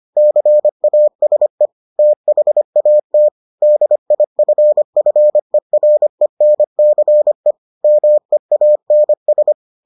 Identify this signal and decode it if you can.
Morse code, heard as a keyed tone, about 25 wpm, 600 Hz. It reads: CASE THAT DIFFERENCE MEANS